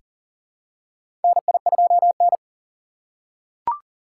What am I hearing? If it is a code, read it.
Morse code, heard as a keyed tone, about 38 wpm, 700 Hz. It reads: NI2N
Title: USA Call Signs - 4 Characters - 4x Character Spacing - 40wpm